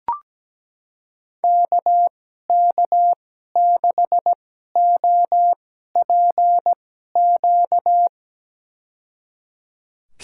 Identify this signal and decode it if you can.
Morse code, heard as a keyed tone, about 17 wpm, spaced as for 12 wpm, 700 Hz. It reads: KK6OPQ
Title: USA Call Signs - 2x Character Spacing - 17wpm